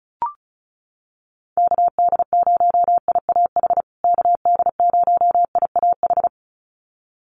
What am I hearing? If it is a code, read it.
Morse code, heard as a keyed tone, about 34 wpm, 700 Hz. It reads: KD0IAH KD0IAH
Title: USA Call Signs Repeated 2x 35wpm